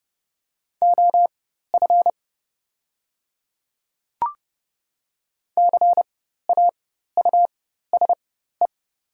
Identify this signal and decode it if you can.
Morse code, heard as a keyed tone, about 30 wpm, spaced as for 14 wpm, 700 Hz. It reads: OF CAUSE
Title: 4x Character Spacing - Top 300 Words - 30wpm